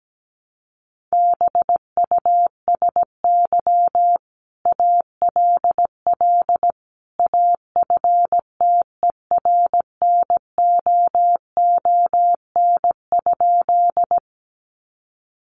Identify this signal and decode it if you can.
Morse code, heard as a keyed tone, about 17 wpm, 700 Hz. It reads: BUSY ALL AFTERNOON?
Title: Sentences from Top 1500 Words 17wpm